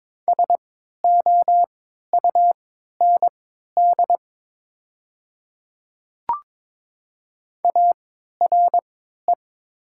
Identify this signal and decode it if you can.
Morse code, heard as a keyed tone, about 22 wpm, spaced as for 13 wpm, 700 Hz. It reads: SOUND ARE
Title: 3x Character Spacing - Top 100 Words - 22wpm